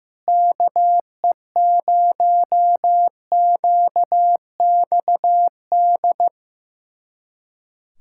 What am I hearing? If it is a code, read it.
Morse code, heard as a keyed tone, about 15 wpm, 700 Hz. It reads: KE0QXD
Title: USA Callsign DE Callsign - 6 Characters - 15wpm